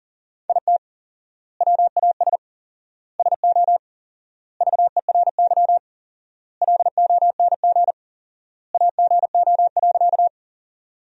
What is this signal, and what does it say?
Morse code, heard as a keyed tone, about 37 wpm, 700 Hz. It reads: IT WAS SO VERY LONG AGO.
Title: 4x Word-Spacing - Sentences from Top 500 Words - 40wpm